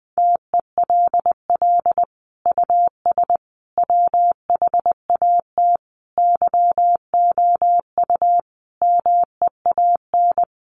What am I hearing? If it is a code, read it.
Morse code, heard as a keyed tone, about 19 wpm, 700 Hz. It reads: TELL US WHAT YOU MEAN
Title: Sentences from Top 1000 Words 20wpm